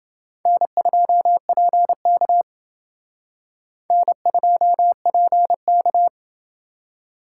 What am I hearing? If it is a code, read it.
Morse code, heard as a keyed tone, about 29 wpm, 700 Hz. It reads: N2PK N2PK